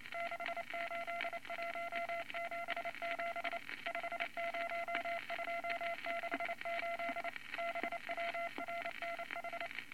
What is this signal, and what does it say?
Morse code, heard as a keyed tone, about 28 wpm, 680 Hz, with noise. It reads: 68177 50268 BWRNH
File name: Morse Code Records / 22 A LCM 172 Twenty Words Per Minute